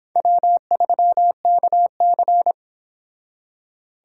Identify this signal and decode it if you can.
Morse code, heard as a keyed tone, about 29 wpm, 700 Hz. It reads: W3KC